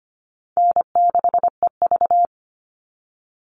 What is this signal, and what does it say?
Morse code, heard as a keyed tone, about 25 wpm, 700 Hz. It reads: N6EV
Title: Call Signs Repeated 3x - 25wpm